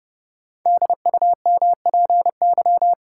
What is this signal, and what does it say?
Morse code, heard as a keyed tone, about 30 wpm, 700 Hz. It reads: DUMPY
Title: Pangram Challenge v2 - 30wpm